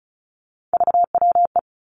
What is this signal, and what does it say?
Morse code, heard as a keyed tone, about 36 wpm, 700 Hz. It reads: VWE
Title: Three Challenging Letter-Number Combinations 35wpm